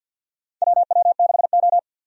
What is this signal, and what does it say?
Morse code, heard as a keyed tone, about 50 wpm, 700 Hz. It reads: WW6O
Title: USA Call Signs - 4 Characters - 50wpm